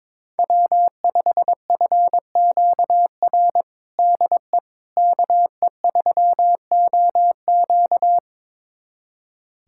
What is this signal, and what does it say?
Morse code, heard as a keyed tone, about 22 wpm, 700 Hz. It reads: W5FQR DE KE3OQ
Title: USA Callsign DE Callsign - 5 Characters - 22wpm